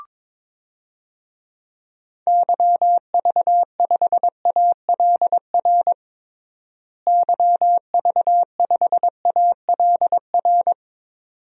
Morse code, heard as a keyed tone, about 22 wpm, 700 Hz. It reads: YV5ALR YV5ALR
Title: DXCC-100 Call Signs - 6 Characters - Repeated 2x - 22wpm